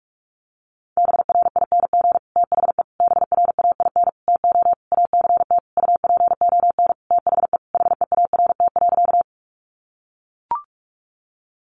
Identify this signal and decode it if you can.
Morse code, heard as a keyed tone, about 44 wpm, 700 Hz. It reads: BRING THE BRAIN TO ACT UPON THE HEART.
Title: Sentences from Top 1200 Words 45wpm